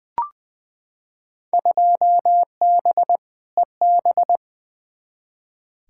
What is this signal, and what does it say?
Morse code, heard as a keyed tone, about 21 wpm, 700 Hz. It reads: 2B EB
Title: Field Day Exchanges - 20wpm